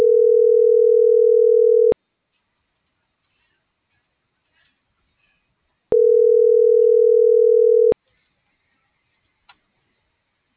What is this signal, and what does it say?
Ringback tone (North America), 440 + 480 Hz.